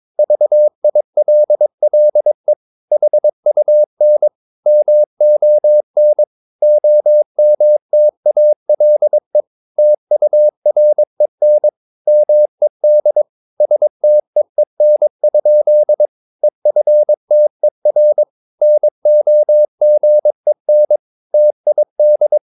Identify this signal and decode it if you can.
Morse code, heard as a keyed tone, about 22 wpm, 600 Hz. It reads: VILLE HUN MON OMTALE TUREN MED STEEN? EFTER NOGEN TID
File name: Ferie QRS 22wpm